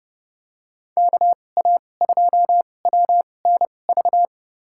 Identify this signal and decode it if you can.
Morse code, heard as a keyed tone, about 32 wpm, spaced as for 22 wpm, 700 Hz. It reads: KA2WNV
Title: USA Call Signs - 6 Characters - 2x Character Spacing - 30wpm